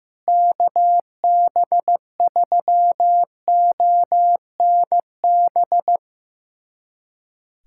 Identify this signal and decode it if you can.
Morse code, heard as a keyed tone, about 16 wpm, 700 Hz. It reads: KB3ONB
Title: USA Callsign DE Callsign - 6 Characters - 15wpm